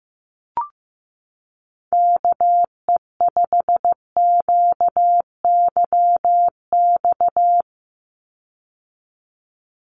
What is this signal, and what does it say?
Morse code, heard as a keyed tone, about 15 wpm, 700 Hz. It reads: KE5QYX
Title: USA Call Signs 15wpm